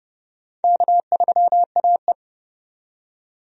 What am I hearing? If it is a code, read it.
Morse code, heard as a keyed tone, about 32 wpm, 700 Hz. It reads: K3AE
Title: USA Call Signs - 4 Characters - 30wpm